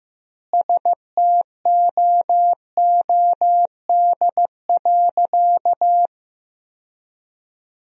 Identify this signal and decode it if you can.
Morse code, heard as a keyed tone, about 15 wpm, 700 Hz. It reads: STOOD.
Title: 4x Word-Spacing - Sentences from Top 500 Words - 15wpm